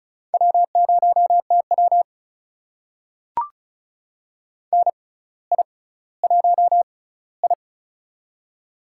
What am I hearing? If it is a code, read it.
Morse code, heard as a keyed tone, about 33 wpm, spaced as for 18 wpm, 700 Hz. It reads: W0TW NI1I
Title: USA Call Signs - 4 Characters - 6x Character Spacing - 35wpm